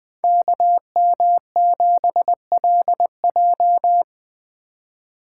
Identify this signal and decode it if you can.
Morse code, heard as a keyed tone, about 20 wpm, 700 Hz. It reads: KM7LJ
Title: Speed Racing - USA 2x2 Call Signs 17wpm